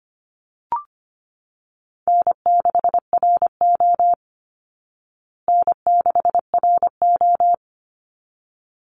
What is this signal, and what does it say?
Morse code, heard as a keyed tone, about 25 wpm, 700 Hz. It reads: N6RO N6RO
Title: Easy Call Signs Repeated 2x - 25wpm